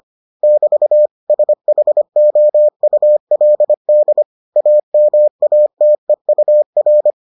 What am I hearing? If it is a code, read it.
Morse code, heard as a keyed tone, about 25 wpm, 600 Hz. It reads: <BT> SHOULD AMATEUR